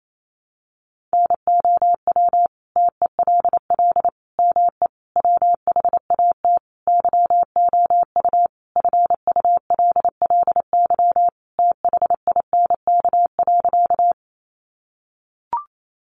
Morse code, heard as a keyed tone, about 28 wpm, 700 Hz. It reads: NOW TELL ME WHAT YOU FULLY THINK.
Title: Sentences from Top 1300 Words 28wpm